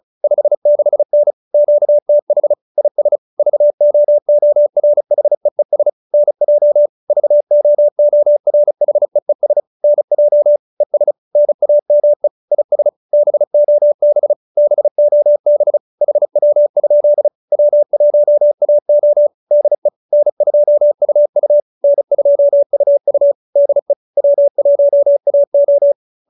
Morse code, heard as a keyed tone, about 34 wpm, 590 Hz. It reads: 56N QTH IS VOORHEES NJ VOORHEES NJ ES NAME IS BOB BOB HW? W1AO DE N2UU N2UU DE W1AO
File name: QSO313 35